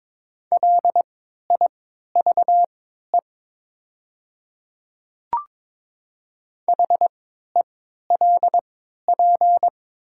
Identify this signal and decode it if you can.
Morse code, heard as a keyed tone, about 22 wpm, spaced as for 13 wpm, 700 Hz. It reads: LIVE HELP
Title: 3x Character Spacing - Top 200 Words - 22wpm